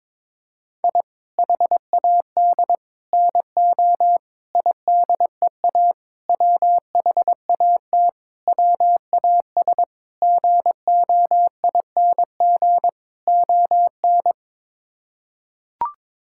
Morse code, heard as a keyed tone, about 22 wpm, 700 Hz. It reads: I HAD NO IDEA WHAT WAS GOING ON
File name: practice / 22wpm / Sets of 8 Words - 22wpm